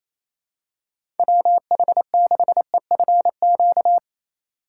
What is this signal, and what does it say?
Morse code, heard as a keyed tone, about 28 wpm, 700 Hz. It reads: WH6EFQ
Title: USA Call Signs - 6 Characters - 2x Character Spacing - 28wpm